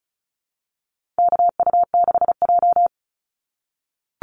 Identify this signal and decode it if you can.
Morse code, heard as a keyed tone, about 34 wpm, 700 Hz. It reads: KU6J
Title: NA SOTA Chaser Callsigns 35wpm